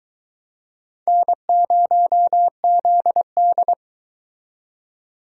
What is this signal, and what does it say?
Morse code, heard as a keyed tone, about 23 wpm, 700 Hz. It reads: N0ZD